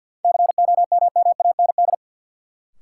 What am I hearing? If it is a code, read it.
Morse code, heard as a keyed tone, about 53 wpm, 700 Hz. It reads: COMMAND
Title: Sentences from Top 1600 Words - New Word Drill - 50wpm